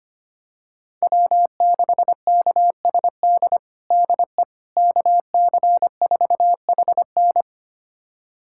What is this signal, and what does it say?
Morse code, heard as a keyed tone, about 25 wpm, 700 Hz. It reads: W6KSD DE KC4HN
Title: USA Callsign DE Callsign - 5 Characters - 25wpm